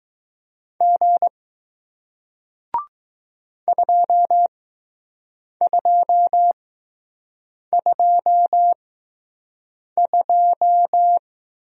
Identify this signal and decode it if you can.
Morse code, heard as a keyed tone, about 20 wpm, spaced as for 6 wpm, 700 Hz. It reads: G 2222